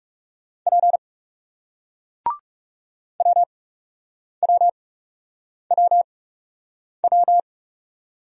Morse code, heard as a keyed tone, about 39 wpm, spaced as for 8 wpm, 700 Hz. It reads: P WWWW